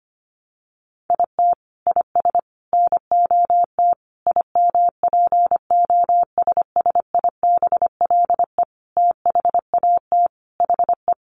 Morse code, heard as a keyed tone, about 25 wpm, 700 Hz. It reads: IT IS NOT IMPOSSIBLE THAT HE
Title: Sentences from Top 1100 Words 25wpm